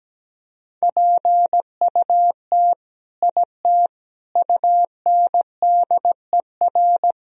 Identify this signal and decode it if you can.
Morse code, heard as a keyed tone, about 17 wpm, 700 Hz. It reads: PUT IT UNDER
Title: Sentences from Top 1000 Words - Review Part 1 - 17wpm